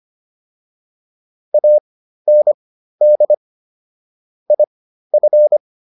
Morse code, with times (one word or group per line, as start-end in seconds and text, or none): 1.5-3.3 AND
4.5-5.6 IF